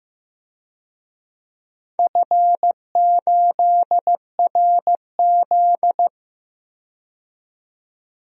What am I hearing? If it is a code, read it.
Morse code, heard as a keyed tone, about 15 wpm, 700 Hz. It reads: F8RZ
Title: DXCC-100 Call Signs - 4 Characters - 15wpm